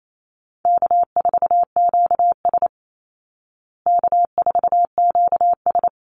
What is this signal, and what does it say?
Morse code, heard as a keyed tone, about 28 wpm, 700 Hz. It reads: K4QS K4QS
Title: Easy Call Signs Repeated 2x - 28wpm